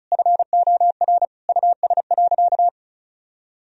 Morse code, heard as a keyed tone, about 34 wpm, 700 Hz. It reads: FOR US.
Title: Sentences from Top 1000 Words - Review Part 2 - 35wpm